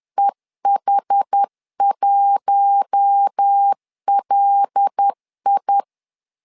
Morse code, heard as a keyed tone, about 10 wpm, 800 Hz. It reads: EH1LI
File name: LEZIONI / 5 LBVC / 5 L 10wpm 2259